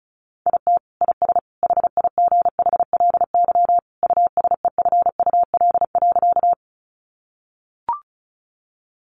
Morse code, heard as a keyed tone, about 34 wpm, 700 Hz. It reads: IT IS HIGHLY USEFUL.
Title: Sentences from Top 1100 Words 35wpm